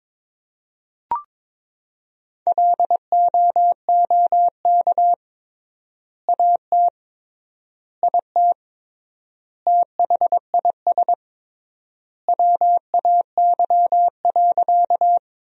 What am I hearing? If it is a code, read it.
Morse code, heard as a keyed tone, about 22 wpm, 700 Hz. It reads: LOOK AT IT THIS WAY.
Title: 3x Word-Spacing - Sentences from Top 100 Words - 22wpm